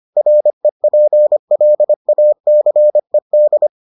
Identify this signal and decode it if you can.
Morse code, heard as a keyed tone, about 25 wpm, 600 Hz. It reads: REPLACED